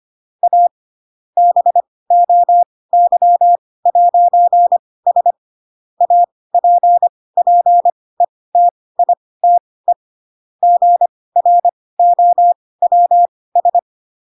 Morse code, heard as a keyed tone, about 25 wpm, spaced as for 18 wpm, 700 Hz. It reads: A BOY'S APPETITE GROWS
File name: Pinocchio / Pinocchio ch5 12 25 ik1hge